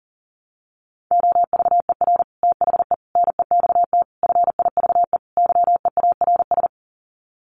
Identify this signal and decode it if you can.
Morse code, heard as a keyed tone, about 40 wpm, 700 Hz. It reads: OVER THE NEXT FIVE YEARS
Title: Sets of 5 Words 40wpm